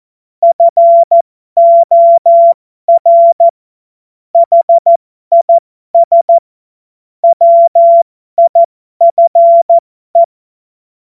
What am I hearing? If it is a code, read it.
Morse code, heard as a keyed tone, about 14 wpm, spaced as for 12 wpm, 680 Hz. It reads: FOR HIS WIFE
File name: Morsekurs / Witze